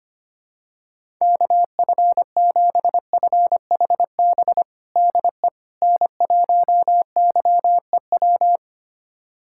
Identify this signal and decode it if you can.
Morse code, heard as a keyed tone, about 25 wpm, 700 Hz. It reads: KF7FHB DE N1YEW
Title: USA Callsign DE Callsign 25wpm